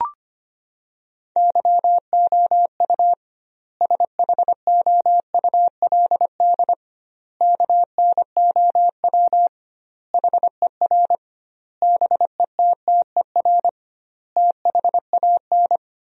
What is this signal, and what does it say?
Morse code, heard as a keyed tone, about 25 wpm, 700 Hz. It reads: YOU SHOULD KNOW HER BETTER THAN
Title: 2x Word-Spacing - Sentences from Top 400 Words - 25wpm